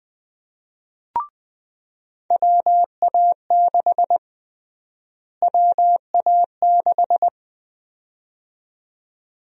Morse code, heard as a keyed tone, about 20 wpm, 700 Hz. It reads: WA6 WA6
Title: USA Call Sign Prefixes - Repeated 2x - 20wpm